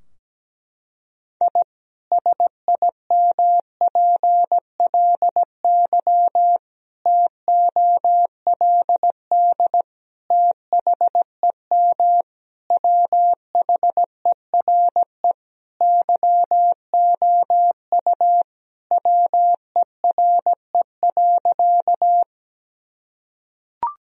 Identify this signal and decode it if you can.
Morse code, heard as a keyed tone, about 17 wpm, 700 Hz. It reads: I SIMPLY TOLD THEM WHERE YOU WERE.
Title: Sentences from Top 1000 Words - Review Part 4 - 17wpm